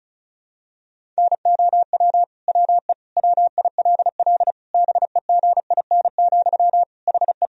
Morse code, heard as a keyed tone, about 34 wpm, 700 Hz. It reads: NOW WE WILL BEGIN, HE
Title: Sentences from Top 1000 Words - Review Part 1 - 35wpm